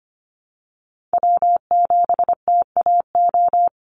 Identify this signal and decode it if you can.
Morse code, heard as a keyed tone, about 25 wpm, 700 Hz. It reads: W7TAO
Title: NA SOTA Chaser Callsigns 25wpm